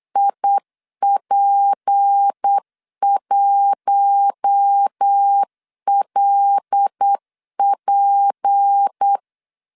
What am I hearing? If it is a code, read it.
Morse code, heard as a keyed tone, about 8 wpm, 800 Hz. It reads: IP1LP
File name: LEZIONI / 6 PKX0 / 6 P 8wpm 3025